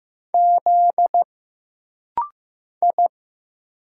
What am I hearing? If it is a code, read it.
Morse code, heard as a keyed tone, about 15 wpm, 700 Hz. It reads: Z I
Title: Single Letter-Number Alt - Rapid-Fire 15wpm